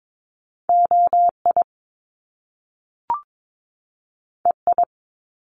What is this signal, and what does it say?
Morse code, heard as a keyed tone, about 22 wpm, 700 Hz. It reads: OI EI
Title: Top Two Letter Combinations in Words 22wpm